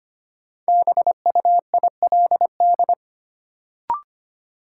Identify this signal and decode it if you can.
Morse code, heard as a keyed tone, about 25 wpm, 700 Hz. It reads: BUILD
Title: Sentences from Top 1000 Words - Review Part 1 - New Word Drill - 25wpm